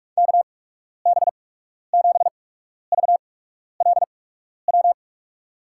Morse code, heard as a keyed tone, about 42 wpm, spaced as for 12 wpm, 700 Hz. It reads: KB7VLW
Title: USA Call Signs - 6 Characters - 8x Character Spacing - 45wpm